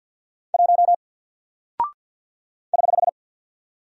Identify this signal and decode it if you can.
Morse code, heard as a keyed tone, about 50 wpm, 700 Hz. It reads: 1 <HH>
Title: Single Character (Letter:Number:Punctuation:Prosign) Alt 50wpm